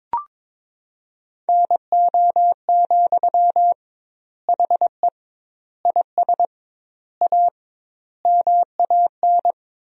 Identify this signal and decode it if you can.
Morse code, heard as a keyed tone, about 22 wpm, 700 Hz. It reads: NO, HE IS A MAN
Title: 2x Word-Spacing - Sentences from Top 400 Words - 22wpm